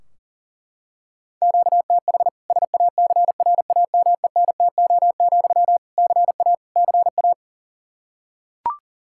Morse code, heard as a keyed tone, about 40 wpm, 700 Hz. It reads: QTH SACRAMENTO, CA CA
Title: Practice QSO - Exchange 1 - Beginning - 40wpm